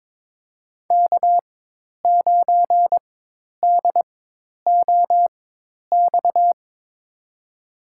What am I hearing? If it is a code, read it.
Morse code, heard as a keyed tone, about 22 wpm, spaced as for 10 wpm, 700 Hz. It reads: K9DOX
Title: USA Call Signs - 5 Characters - 4x Character Spacing - 22wpm